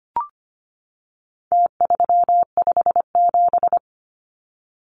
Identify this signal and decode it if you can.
Morse code, heard as a keyed tone, about 25 wpm, 700 Hz. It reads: T357
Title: Numbers - Contest Style - 4-Digit 25wpm